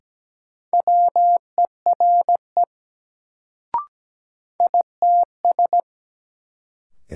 Morse code, heard as a keyed tone, about 17 wpm, 700 Hz. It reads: WERE ITS
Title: Sets of 2 Words - New Word Drill - 17wpm